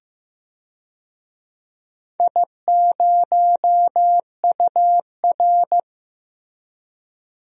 Morse code, heard as a keyed tone, about 15 wpm, 700 Hz. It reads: I0UR
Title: DXCC-100 Call Signs - 4 Characters - Repeated 2x - 15wpm